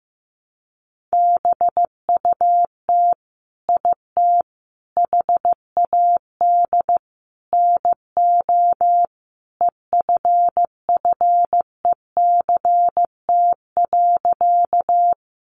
Morse code, heard as a keyed tone, about 15 wpm, 700 Hz. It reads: BUT IT HAD NO EFFECT.
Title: Sentences from Top 1000 Words 15wpm